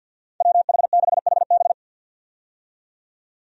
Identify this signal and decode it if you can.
Morse code, heard as a keyed tone, about 51 wpm, 700 Hz. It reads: WH6HB
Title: USA Call Signs - 5 Characters - 50wpm